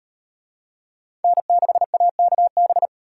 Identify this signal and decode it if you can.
Morse code, heard as a keyed tone, about 35 wpm, 700 Hz. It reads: N6AKB